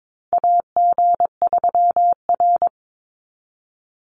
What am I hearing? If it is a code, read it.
Morse code, heard as a keyed tone, about 22 wpm, 700 Hz. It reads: AG3R
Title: Call Signs Repeated 3x - 22wpm